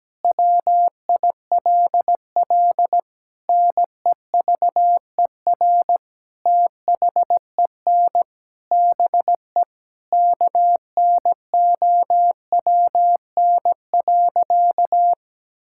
Morse code, heard as a keyed tone, about 17 wpm, 700 Hz. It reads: WILL NEVER THEN BE KNOWN.
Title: Sentences from Top 1600 Words 17wpm